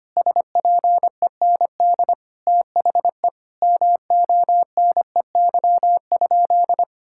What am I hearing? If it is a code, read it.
Morse code, heard as a keyed tone, about 25 wpm, 700 Hz. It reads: SPEND THE MONEY?
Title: Sentences from Top 1000 Words - Review Part 5 - 25wpm